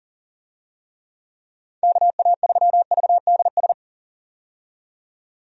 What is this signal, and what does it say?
Morse code, heard as a keyed tone, about 37 wpm, 700 Hz. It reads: KA3VDS
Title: USA Call Signs - 6 Characters - 40wpm